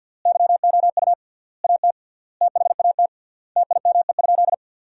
Morse code, heard as a keyed tone, about 50 wpm, 700 Hz. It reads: YOU AT THAT TIME?
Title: 3x Word-Spacing - Sentences from Top 200 Words - 50wpm